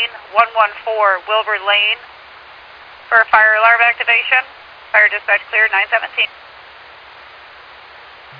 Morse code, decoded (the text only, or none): none